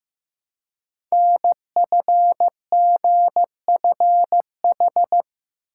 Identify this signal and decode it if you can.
Morse code, heard as a keyed tone, about 15 wpm, 700 Hz. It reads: NFGFH
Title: Code Groups - 5 Letters - 15wpm